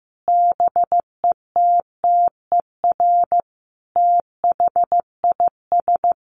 Morse code, heard as a keyed tone, about 15 wpm, 700 Hz. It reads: BETTER THIS
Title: Sentences from Top 1500 Words 15wpm